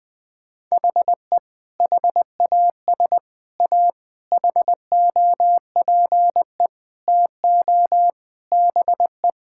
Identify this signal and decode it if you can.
Morse code, heard as a keyed tone, about 20 wpm, 700 Hz. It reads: HE HAS A HOPE TO BE
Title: Sentences from Top 1000 Words - Review Part 6 - 20wpm